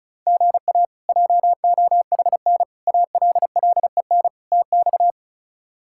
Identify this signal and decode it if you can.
Morse code, heard as a keyed tone, about 34 wpm, 700 Hz. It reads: GA JOHN ALLEN TX
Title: K1USN SST QSO Practice - 35wpm